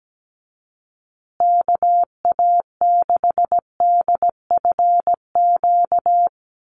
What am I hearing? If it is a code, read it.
Morse code, heard as a keyed tone, about 17 wpm, 700 Hz. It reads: KA6DFQ
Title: USA Call Signs Repeated 2x 17wpm